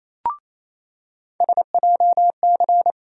Morse code, heard as a keyed tone, about 28 wpm, 700 Hz. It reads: SJC
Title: USA Call Sign Postfixes - 3 Letters 28wpm